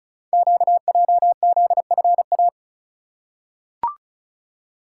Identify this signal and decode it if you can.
Morse code, heard as a keyed tone, about 36 wpm, 700 Hz. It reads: QJZFA
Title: Code Groups - 5 Letters - 35wpm